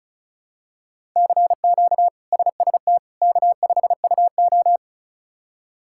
Morse code, heard as a keyed tone, about 36 wpm, 700 Hz. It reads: CQ SST K5UO
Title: K1USN SST QSO Practice - 35wpm